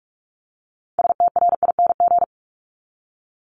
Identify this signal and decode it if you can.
Morse code, heard as a keyed tone, about 46 wpm, 700 Hz. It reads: STRING
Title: Multi-letter Phonemes 45wpm